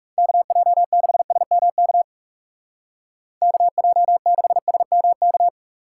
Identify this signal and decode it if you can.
Morse code, heard as a keyed tone, about 40 wpm, 700 Hz. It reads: KJ6SMK KJ6SMK